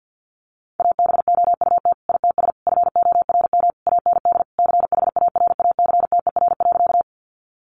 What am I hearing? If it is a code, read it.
Morse code, heard as a keyed tone, about 50 wpm, 700 Hz. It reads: ABOUT ITS FORM AND CHARACTER.
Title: Sentences from Top 1200 Words 50wpm